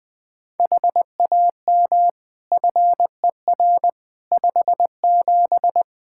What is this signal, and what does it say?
Morse code, heard as a keyed tone, about 20 wpm, 700 Hz. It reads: HAM FER 57
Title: K1USN SST QSO Practice 20wpm